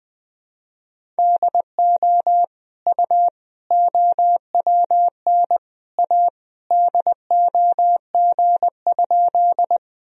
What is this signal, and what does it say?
Morse code, heard as a keyed tone, about 20 wpm, 700 Hz. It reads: DO U OWN A DOG?
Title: Sentences from Top 400 Words and Abbreviations 20wpm